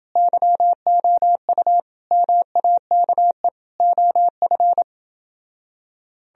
Morse code, heard as a keyed tone, about 29 wpm, 700 Hz. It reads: YOU MAKE OF